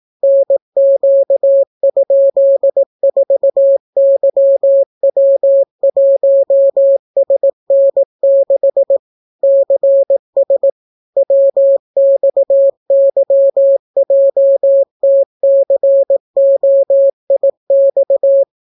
Morse code, heard as a keyed tone, about 18 wpm, 550 Hz. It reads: NQ?4YW1SN6 CS WXYJTCOIX